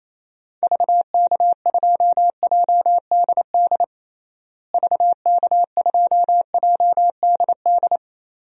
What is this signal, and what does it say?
Morse code, heard as a keyed tone, about 28 wpm, 700 Hz. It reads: VK2JDD VK2JDD
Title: DXCC-100 Call Signs - 6 Characters - Repeated 2x - 28wpm